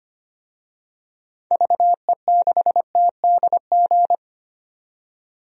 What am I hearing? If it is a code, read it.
Morse code, heard as a keyed tone, about 25 wpm, 700 Hz. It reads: VE6TDG
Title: DXCC-100 Call Signs - 6 Characters - 25wpm